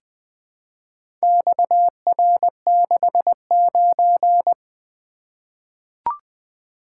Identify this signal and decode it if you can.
Morse code, heard as a keyed tone, about 20 wpm, 700 Hz. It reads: XR69
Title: Four Letter-Number Combinations - 20wpm